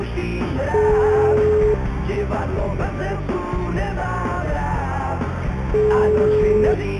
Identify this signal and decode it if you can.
Ringback tone (Europe, 425 Hz), with noise.